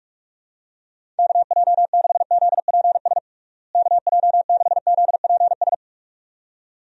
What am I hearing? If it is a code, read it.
Morse code, heard as a keyed tone, about 41 wpm, 700 Hz. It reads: KJ6ZPS KJ6ZPS
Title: USA Call Signs - 6 Characters - Repeated 2x - 45wpm